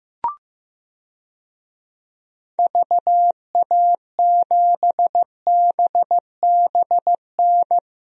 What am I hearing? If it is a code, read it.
Morse code, heard as a keyed tone, about 15 wpm, 700 Hz. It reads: VA7BBN
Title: DXCC-100 Call Signs - 6 Characters - Repeated 3x - 15wpm